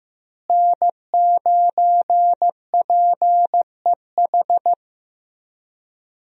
Morse code, heard as a keyed tone, about 15 wpm, 700 Hz. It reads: N9PEH
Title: USA Call Signs - 5 Characters - Repeated 2x - 15wpm